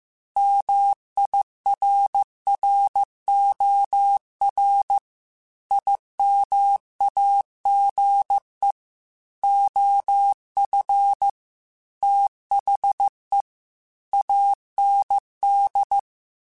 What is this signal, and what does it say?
Morse code, heard as a keyed tone, about 15 wpm, 800 Hz. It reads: MIRROR IMAGE OF THE AND
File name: Technical Subjects / Technical Subjects-mp3s / OR Gate @13